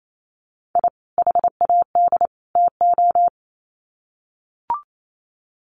Morse code, heard as a keyed tone, about 28 wpm, 700 Hz. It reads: I HAD TO
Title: Sets of 3 Words 28wpm